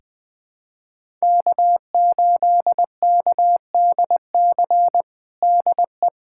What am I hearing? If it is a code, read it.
Morse code, heard as a keyed tone, about 21 wpm, 700 Hz. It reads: K8KDC DE
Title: USA Callsign DE Callsign - 5 Characters - 20wpm